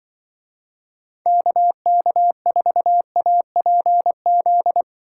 Morse code, heard as a keyed tone, about 24 wpm, 700 Hz. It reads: KK4APZ
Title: Speed Racing - USA 2x3 Call Signs 20wpm